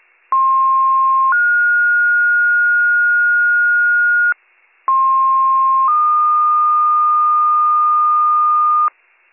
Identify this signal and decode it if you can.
Two-tone page, heard as a one-second tone, then a three-second tone, about 1060 then 1430 Hz, about 1060 then 1220 Hz, with noise. It is heard from 0.3 to 8.9 s.